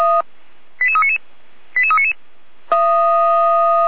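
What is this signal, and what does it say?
Five-tone selective call (ZVEI), heard as quick runs of five tones, tones of about 70 ms. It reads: 88399 88399
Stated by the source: Received on SDR as a ZVEI five-tone selective call.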